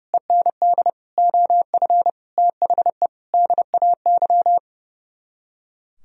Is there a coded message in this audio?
Yes: Morse code from 0.1 to 4.6 s, heard as a keyed tone, about 38 wpm, 700 Hz. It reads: END OF THE DAY